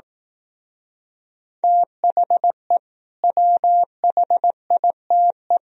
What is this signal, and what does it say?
Morse code, heard as a keyed tone, about 18 wpm, 700 Hz. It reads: THE WHITE
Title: Speed-Racing - Sets of 3 Words 15wpm